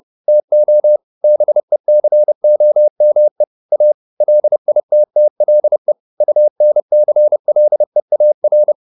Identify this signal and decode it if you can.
Morse code, heard as a keyed tone, about 30 wpm, 600 Hz. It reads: TO BECOME A LITTLE UNCLEAR